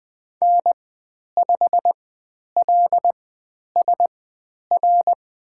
Morse code, heard as a keyed tone, about 20 wpm, spaced as for 10 wpm, 700 Hz. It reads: N5LSR
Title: 095 Lesson 16g - Letter c - Call Signs 20wpm